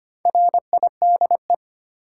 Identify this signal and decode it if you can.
Morse code, heard as a keyed tone, about 25 wpm, 700 Hz. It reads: RIDE
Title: Sentences from Top 1200 Words - New Word Drill Alt - 25wpm